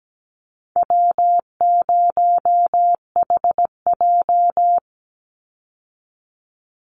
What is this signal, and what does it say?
Morse code, heard as a keyed tone, about 17 wpm, 700 Hz. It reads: W0HJ
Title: USA Call Signs 17wpm